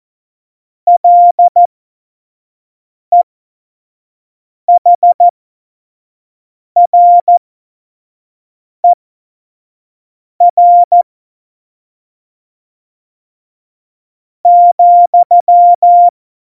Morse code, heard as a keyed tone, about 14 wpm, spaced as for 4 wpm, 700 Hz. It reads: LEHRER,
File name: Morsekurs / Witze / Witz 30. April 2019, 25 F70